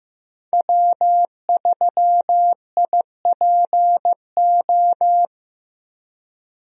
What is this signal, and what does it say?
Morse code, heard as a keyed tone, about 15 wpm, 700 Hz. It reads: W3IPO
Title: USA Call Signs - 5 Characters - Repeated 3x - 15wpm